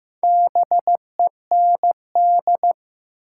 Morse code, heard as a keyed tone, about 15 wpm, 700 Hz. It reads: BEND
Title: Four Letter Words - 15wpm